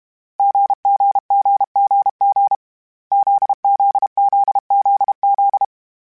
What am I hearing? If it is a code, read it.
Morse code, heard as a keyed tone, about 32 wpm, 800 Hz. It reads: GGGGG ZZZZZ
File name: Short Runs / Short Runs-mp3s / GZ GZ7 @30